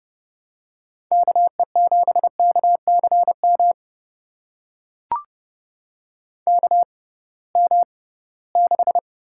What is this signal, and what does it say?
Morse code, heard as a keyed tone, about 32 wpm, spaced as for 20 wpm, 700 Hz. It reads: KE7KCM KM6
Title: USA Call Signs - 6 Characters - 6x Character Spacing - 30wpm